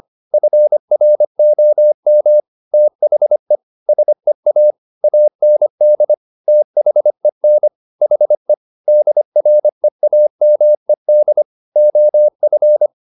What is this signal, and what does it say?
Morse code, heard as a keyed tone, about 25 wpm, 600 Hz. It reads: FROM THE SEA AND THEN HE DREAMED OF
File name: the old man and the sea part 1 25wpm